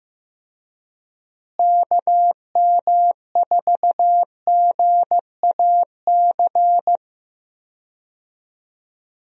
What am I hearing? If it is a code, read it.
Morse code, heard as a keyed tone, about 15 wpm, 700 Hz. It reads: KM4GAC
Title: USA Call Signs - 6 Characters - Repeated 3x - 15wpm